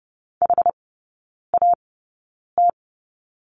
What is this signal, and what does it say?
Morse code, heard as a keyed tone, about 30 wpm, spaced as for 9 wpm, 700 Hz. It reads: HAT
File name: Lesson 18e - Letter m - Three Characters 30wpm